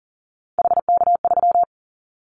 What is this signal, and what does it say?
Morse code, heard as a keyed tone, about 40 wpm, 700 Hz. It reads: HK3
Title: Callsign Prefixes for Top 100 DXCC - 40wpm